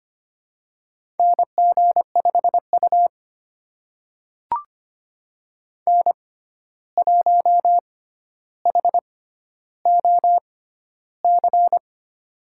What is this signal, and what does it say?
Morse code, heard as a keyed tone, about 25 wpm, spaced as for 12 wpm, 700 Hz. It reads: NG5U N1HOC
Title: USA Call Signs - 6x Character Spacing - 25wpm